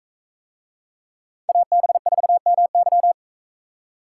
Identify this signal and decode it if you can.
Morse code, heard as a keyed tone, about 42 wpm, 700 Hz. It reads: AB4MY